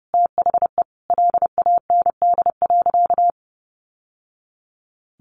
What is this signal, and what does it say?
Morse code, heard as a keyed tone, about 30 wpm, 700 Hz. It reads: THE LAND.
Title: Sentences from Top 1200 Words 30wpm